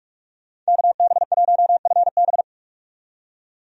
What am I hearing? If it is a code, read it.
Morse code, heard as a keyed tone, about 46 wpm, 700 Hz. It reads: KB1FB
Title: USA Call Signs - 5 Characters - 45wpm